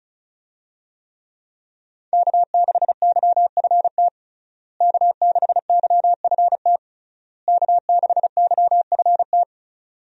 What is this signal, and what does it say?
Morse code, heard as a keyed tone, about 36 wpm, 700 Hz. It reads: K6YFT K6YFT K6YFT
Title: USA Call Signs - 5 Characters - Repeated 3x - 35wpm